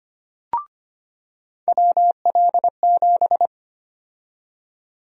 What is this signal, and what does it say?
Morse code, heard as a keyed tone, about 25 wpm, 700 Hz. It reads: WL7
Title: USA Call Sign Prefixes - 25wpm